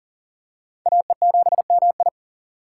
Morse code, heard as a keyed tone, about 40 wpm, 690 Hz. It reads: AE7MI
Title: USA Call Signs - 5 Characters - 3x Character Spacing - 40wpm